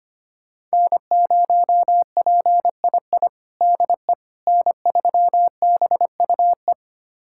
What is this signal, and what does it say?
Morse code, heard as a keyed tone, about 25 wpm, 700 Hz. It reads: N0PII DE N3BUE
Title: USA Callsign DE Callsign - 5 Characters - 25wpm